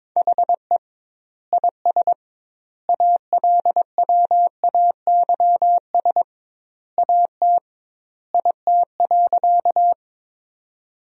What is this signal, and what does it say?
Morse code, heard as a keyed tone, about 22 wpm, 700 Hz. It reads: HE IS ALWAYS AT IT.
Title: 2x Word-Spacing - Sentences from Top 300 Words - 22wpm